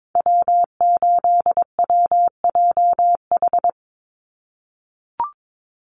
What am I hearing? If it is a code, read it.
Morse code, heard as a keyed tone, about 22 wpm, 700 Hz. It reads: W8WJH
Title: USA Call Signs 22wpm